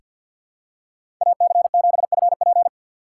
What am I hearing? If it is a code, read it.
Morse code, heard as a keyed tone, about 50 wpm, 700 Hz. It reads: AC7FP
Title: USA Call Signs - 5 Characters - 6x Character Spacing - 50wpm